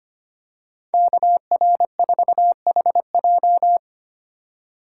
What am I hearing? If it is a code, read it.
Morse code, heard as a keyed tone, about 25 wpm, 700 Hz. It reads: KR4HJ
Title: USA Call Signs - 5 Characters - 2x Character Spacing - 25wpm